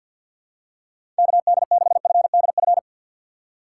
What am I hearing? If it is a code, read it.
Morse code, heard as a keyed tone, about 52 wpm, 700 Hz. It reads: KD6FDF
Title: USA Call Signs - 6 Characters - 6x Character Spacing - 50wpm